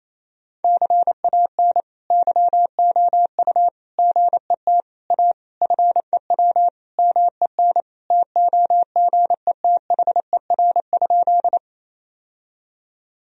Morse code, heard as a keyed tone, about 27 wpm, 700 Hz. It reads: CAN YOU GET A FEW MEN TOGETHER?
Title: Sentences from Top 1000 Words - Review Part 1 - 28wpm